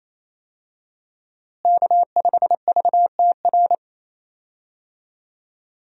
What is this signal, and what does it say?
Morse code, heard as a keyed tone, about 27 wpm, 700 Hz. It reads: K5VTR
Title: USA Call Signs - 5 Characters - Repeated 2x - 28wpm